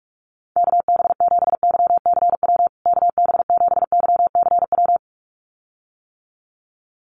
Morse code, heard as a keyed tone, about 44 wpm, 700 Hz. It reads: KB7YCW KB7YCW
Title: USA Call Signs Repeated 2x 45wpm